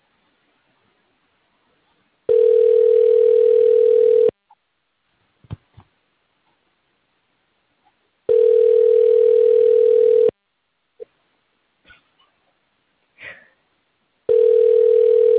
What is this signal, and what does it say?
Ringback tone (North America), 440 + 480 Hz.